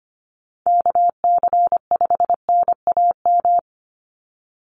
Morse code, heard as a keyed tone, about 25 wpm, 700 Hz. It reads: KC5NAM
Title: USA Call Signs 25wpm